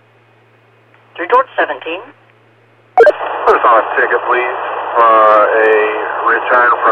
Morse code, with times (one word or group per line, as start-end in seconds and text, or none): none